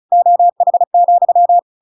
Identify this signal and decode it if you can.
Morse code, heard as a keyed tone, about 34 wpm, 700 Hz. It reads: OH,
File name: Pinocchio / Pinocchio ch7 35 35 ik1hge